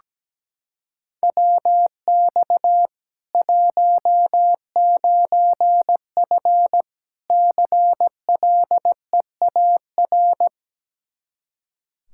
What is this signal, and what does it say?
Morse code, heard as a keyed tone, about 17 wpm, 700 Hz. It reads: WX 19F CLEAR
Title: Practice QSO - Exchange 2 - Info - 17wpm